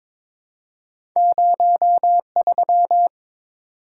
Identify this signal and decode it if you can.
Morse code, heard as a keyed tone, about 22 wpm, 700 Hz. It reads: 03
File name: practice / 22wpm / Two Letter-Number Combinations - 22wpm